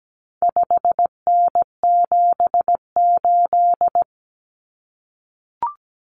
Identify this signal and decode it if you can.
Morse code, heard as a keyed tone, about 17 wpm, 700 Hz. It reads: 5N78
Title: Common Cut Numbers - 4-Digit 17wpm